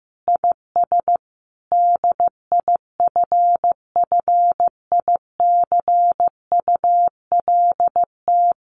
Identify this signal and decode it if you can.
Morse code, heard as a keyed tone, about 14 wpm, 700 Hz. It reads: IS DIFFICULT
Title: Sentences from Top 1500 Words 15wpm